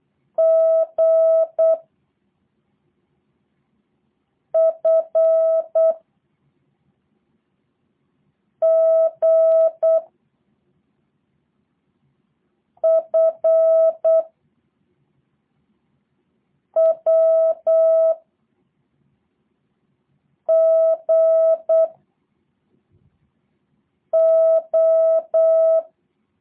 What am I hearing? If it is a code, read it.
Morse code, heard as a keyed tone, about 8 wpm, 650 Hz. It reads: G F G F W G O
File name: jh8cbh / 2023cw / review1-4